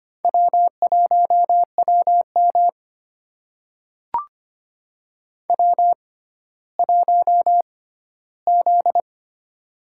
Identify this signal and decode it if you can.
Morse code, heard as a keyed tone, about 26 wpm, spaced as for 14 wpm, 700 Hz. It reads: W1WM W1Z
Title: USA Call Signs - 4 Characters - 6x Character Spacing - 25wpm